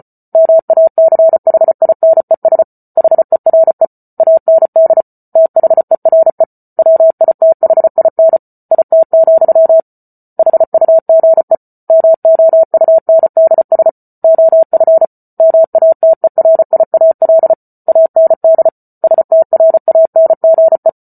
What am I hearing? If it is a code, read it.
Morse code, heard as a keyed tone, about 34 wpm, 650 Hz. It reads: MACHINES HERE AND THERE WITHIN IT, HUGE MOUNDS OF MATERIAL AND STRANGE